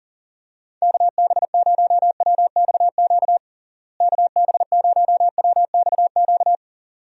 Morse code, heard as a keyed tone, about 39 wpm, 700 Hz. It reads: KB0WXQ KB0WXQ
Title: USA Call Signs - 6 Characters - Repeated 2x - 40wpm